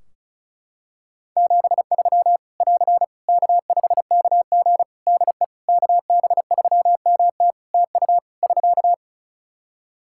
Morse code, heard as a keyed tone, about 34 wpm, 700 Hz. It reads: 73 <AR> K5KG DE KB3MT TU <SK>
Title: K1USN SST QSO Practice 35wpm